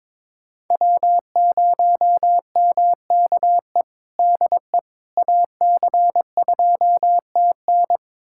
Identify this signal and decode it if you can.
Morse code, heard as a keyed tone, about 22 wpm, 700 Hz. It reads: W0MKE DE AC2TN
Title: USA Callsign DE Callsign - 5 Characters - 22wpm